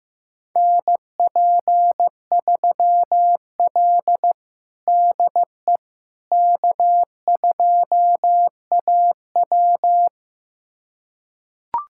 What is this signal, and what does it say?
Morse code, heard as a keyed tone, about 15 wpm, 700 Hz. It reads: NP3L DE K2AW
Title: USA Callsign DE Callsign - 4 Characters - 15wpm